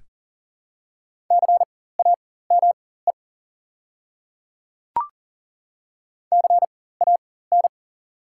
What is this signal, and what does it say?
Morse code, heard as a keyed tone, about 39 wpm, spaced as for 18 wpm, 690 Hz. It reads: CAME CAN
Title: 4x Character Spacing - Top 500 Words - 40wpm